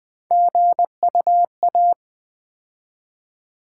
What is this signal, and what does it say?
Morse code, heard as a keyed tone, about 20 wpm, 700 Hz. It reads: GUA
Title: Build A Word - 3 Letters At A Time - 9-Letter Words - 20wpm